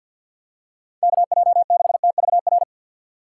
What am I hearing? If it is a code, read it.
Morse code, heard as a keyed tone, about 54 wpm, 700 Hz. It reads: KJ6TVR
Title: USA Call Signs - 6 Characters - 6x Character Spacing - 50wpm